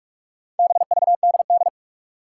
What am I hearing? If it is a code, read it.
Morse code, heard as a keyed tone, about 46 wpm, 700 Hz. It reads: BVDD
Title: Four Challenging Letter-Number Combinations 45wpm